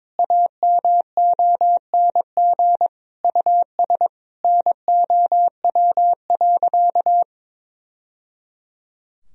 Morse code, heard as a keyed tone, about 22 wpm, 700 Hz. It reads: AMONG US NOW.
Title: Sentences from Top 1000 Words - Review Part 3 - 22wpm